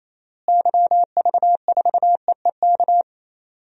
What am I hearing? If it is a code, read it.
Morse code, heard as a keyed tone, about 28 wpm, 700 Hz. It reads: YV4EEK
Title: DXCC-100 Call Signs - 6 Characters - Repeated 2x - 28wpm